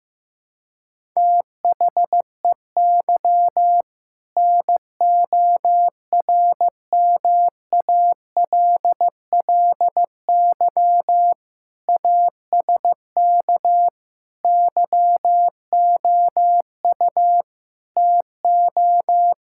Morse code, heard as a keyed tone, about 15 wpm, 700 Hz. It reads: THEY NORMALLY ASK YOU TO
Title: Sentences from Top 1000 Words - Review Part 6 - 15wpm